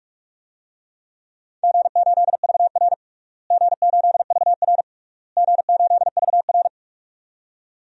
Morse code, heard as a keyed tone, about 46 wpm, 700 Hz. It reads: G8VR G8VR G8VR
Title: DXCC-100 Call Signs - 4 Characters - Repeated 3x - 45wpm